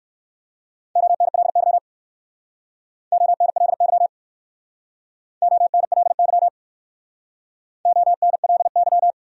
Morse code, heard as a keyed tone, about 53 wpm, 700 Hz. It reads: ONLY ONLY ONLY ONLY
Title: Speed-Racing Top 200 Words 45wpm